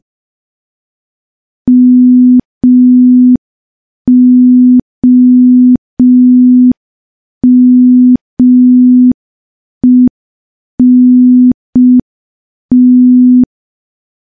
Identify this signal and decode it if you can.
Morse code, heard as a keyed tone, about 5 wpm, 250 Hz. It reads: MOMENT
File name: morse / morse-training-05WPM